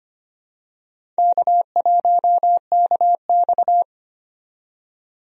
Morse code, heard as a keyed tone, about 25 wpm, 700 Hz. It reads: K1KX
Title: USA Call Signs - 4 Characters - 25wpm